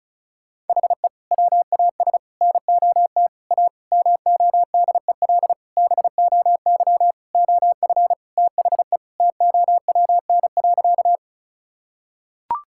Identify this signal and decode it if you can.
Morse code, heard as a keyed tone, about 36 wpm, 700 Hz. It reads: HE WAS NOT A MODEL BOY OF THE TOWN.
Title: Sentences from Top 1000 Words - Review Part 5 - 35wpm